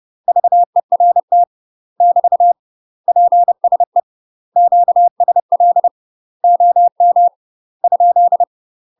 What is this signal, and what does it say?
Morse code, heard as a keyed tone, about 30 wpm, 700 Hz. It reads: VERT <BT> PSE QSL OM ?
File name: UK MASTER QSOs TRAINING / 30WPM / UK MASTER QSOs 0014